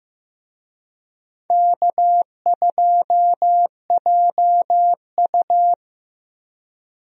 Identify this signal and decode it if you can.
Morse code, heard as a keyed tone, about 15 wpm, 700 Hz. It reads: K2JU
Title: USA Call Signs - 4 Characters - Repeated 2x - 15wpm